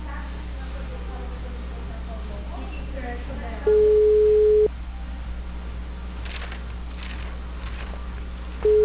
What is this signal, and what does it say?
Ringback tone (Europe, 425 Hz), with speech.
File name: out- -2002- - - .45949